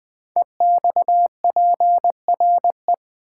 Morse code, heard as a keyed tone, about 20 wpm, 700 Hz. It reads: EXPRE
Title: Build A Word - 5 Letters At A Time - 10-Letter Words - 20wpm